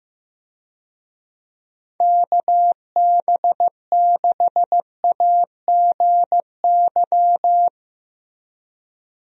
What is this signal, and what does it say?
Morse code, heard as a keyed tone, about 15 wpm, 700 Hz. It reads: KB6AGY
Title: USA Call Signs - 6 Characters - Repeated 3x - 15wpm